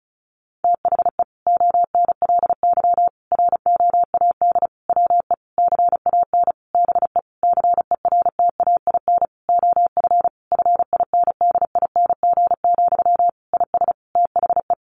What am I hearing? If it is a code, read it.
Morse code, heard as a keyed tone, about 34 wpm, 700 Hz. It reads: THE ONLY ROAD WE CAN BE CERTAIN OF FINDING, IS THE
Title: Sentences from Top 1100 Words 35wpm